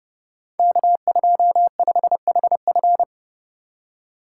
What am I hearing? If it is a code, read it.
Morse code, heard as a keyed tone, about 30 wpm, 700 Hz. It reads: K25HF
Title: Code Groups - 5 Letter:Numbers - 30wpm